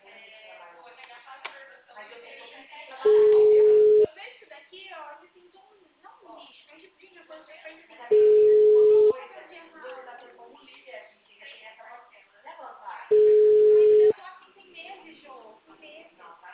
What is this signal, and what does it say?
Ringback tone (Europe, 425 Hz), with speech.
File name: out- -2003- - - .59